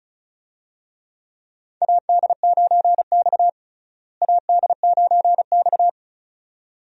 Morse code, heard as a keyed tone, about 36 wpm, 700 Hz. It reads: AD9X AD9X
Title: USA Call Signs - 4 Characters - Repeated 2x - 35wpm